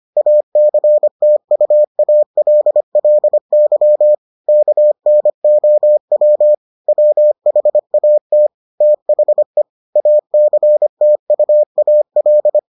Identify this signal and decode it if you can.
Morse code, heard as a keyed tone, about 25 wpm, 600 Hz. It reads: ACTUALLY KNOW WHAT THE ACTUAL